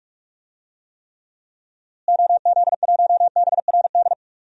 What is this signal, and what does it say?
Morse code, heard as a keyed tone, about 43 wpm, 700 Hz. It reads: OZ1BRD
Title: DXCC-100 Call Signs - 6 Characters - Repeated 3x - 45wpm